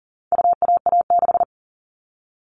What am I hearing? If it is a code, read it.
Morse code, heard as a keyed tone, about 40 wpm, 700 Hz. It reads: UAA6
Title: All Cut Numbers - 4-Digit 40wpm